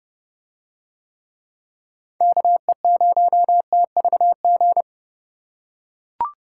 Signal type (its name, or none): Morse code